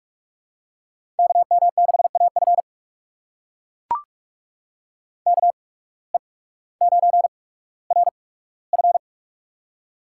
Morse code, heard as a keyed tone, about 44 wpm, spaced as for 19 wpm, 700 Hz. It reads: KM6AF KE9RF
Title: USA Call Signs - 5 Characters - 8x Character Spacing - 45wpm